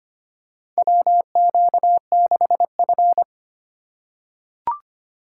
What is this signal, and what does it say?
Morse code, heard as a keyed tone, about 25 wpm, 700 Hz. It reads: WQ6F
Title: USA Call Signs - 2x Character Spacing - 25wpm